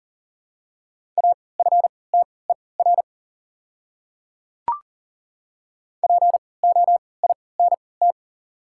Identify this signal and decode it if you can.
Morse code, heard as a keyed tone, about 40 wpm, spaced as for 23 wpm, 690 Hz. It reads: AFTER POINT
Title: 3x Character Spacing - Top 300 Words - 40wpm